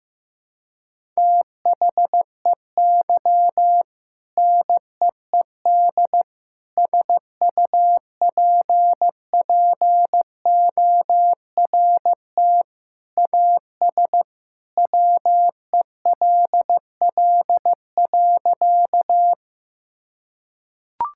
Morse code, heard as a keyed tone, about 15 wpm, 700 Hz. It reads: THEY NEED SUPPORT AS WELL.
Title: Sentences from Top 1000 Words - Review Part 5 - 15wpm